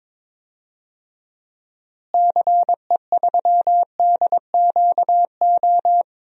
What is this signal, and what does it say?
Morse code, heard as a keyed tone, about 22 wpm, 700 Hz. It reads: CE3DQO
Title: DXCC-100 Call Signs - 6 Characters - Repeated 2x - 22wpm